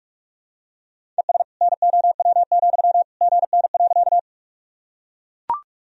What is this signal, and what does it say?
Morse code, heard as a keyed tone, about 41 wpm, 700 Hz. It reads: ES NOW, GN.
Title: Sentences from Top 300 Words and Abbreviations 45wpm